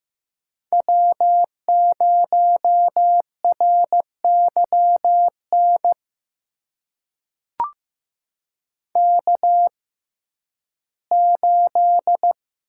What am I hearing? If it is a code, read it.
Morse code, heard as a keyed tone, about 15 wpm, spaced as for 11 wpm, 700 Hz. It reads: W0RYN K8
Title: USA Call Signs - 5 Characters - 6x Character Spacing - 15wpm